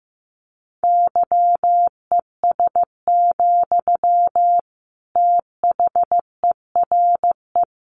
Morse code, heard as a keyed tone, about 15 wpm, 700 Hz. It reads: YES, THERE
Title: Sentences from Top 500 Words 15wpm